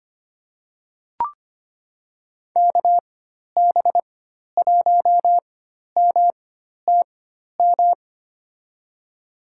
Morse code, heard as a keyed tone, about 25 wpm, spaced as for 12 wpm, 700 Hz. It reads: KB1MTM
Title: USA Call Signs - 4x Character Spacing - 25wpm